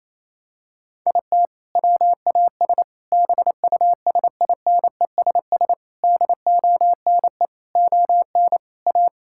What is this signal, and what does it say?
Morse code, heard as a keyed tone, about 28 wpm, 700 Hz. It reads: IT WAS BUSINESS DONE ON A
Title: Sentences from Top 1000 Words - Review Part 5 - 28wpm